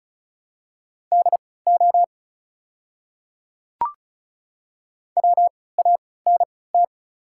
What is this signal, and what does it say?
Morse code, heard as a keyed tone, about 36 wpm, spaced as for 20 wpm, 700 Hz. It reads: DO WANT
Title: 3x Character Spacing - Top 200 Words - 35wpm